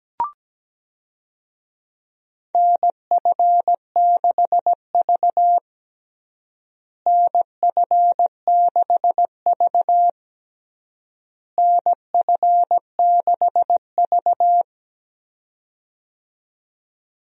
Morse code, heard as a keyed tone, about 17 wpm, 700 Hz. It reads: NF6V NF6V NF6V
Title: USA Call Signs - 4 Characters - Repeated 3x - 17wpm